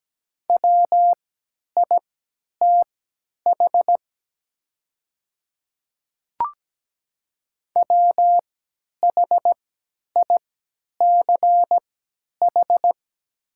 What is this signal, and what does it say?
Morse code, heard as a keyed tone, about 17 wpm, spaced as for 10 wpm, 700 Hz. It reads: WITH WHICH
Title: 3x Character Spacing - Top 100 Words - 17wpm